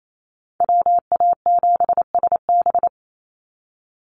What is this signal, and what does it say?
Morse code, heard as a keyed tone, about 28 wpm, 700 Hz. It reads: WA7SB
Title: NA SOTA Chaser Callsigns 28wpm